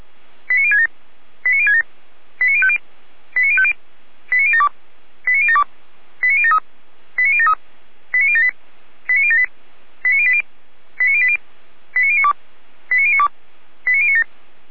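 Five-tone selective call (ZVEI), heard as quick runs of five tones, tones of about 80 ms. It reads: 89067 89067 89050 89050 89072 89072 89073 89073 89078 89078 89080 89080 89092 89092 89097
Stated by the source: ZVEI five-tone selective call received on SDR.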